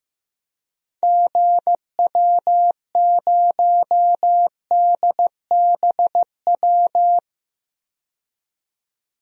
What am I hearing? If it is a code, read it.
Morse code, heard as a keyed tone, about 15 wpm, 700 Hz. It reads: GW0DBW
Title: DXCC-100 Call Signs - 6 Characters - 15wpm